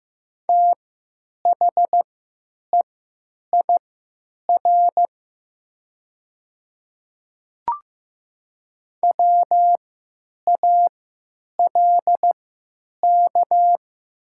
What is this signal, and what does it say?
Morse code, heard as a keyed tone, about 15 wpm, spaced as for 9 wpm, 700 Hz. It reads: THEIR WALK
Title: 3x Character Spacing - Top 300 Words - 15wpm